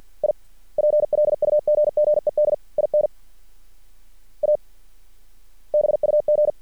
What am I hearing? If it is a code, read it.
Morse code, heard as a keyed tone, about 50 wpm, 600 Hz. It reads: I PLUGGED IN A BUG
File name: archived / CRSnet